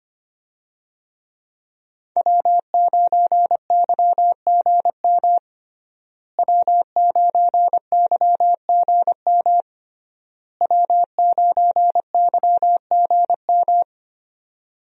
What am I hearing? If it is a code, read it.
Morse code, heard as a keyed tone, about 25 wpm, 700 Hz. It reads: W9YGM W9YGM W9YGM
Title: USA Call Signs - 5 Characters - Repeated 3x - 25wpm